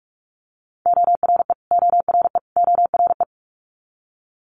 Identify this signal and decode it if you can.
Morse code, heard as a keyed tone, about 46 wpm, 700 Hz. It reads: ORE ORE ORE
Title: Multi-letter Phonemes Repeated 3x 45wpm